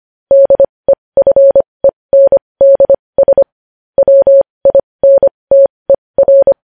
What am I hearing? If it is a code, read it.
Morse code, heard as a keyed tone, about 25 wpm, spaced as for 20 wpm, 550 Hz. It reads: DEFENDS WINTER
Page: News Headlines in Morse Code at 20 WPM